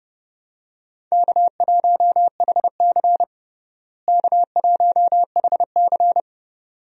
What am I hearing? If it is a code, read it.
Morse code, heard as a keyed tone, about 30 wpm, 700 Hz. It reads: K1HC K1HC
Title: USA Call Signs - 4 Characters - Repeated 2x - 30wpm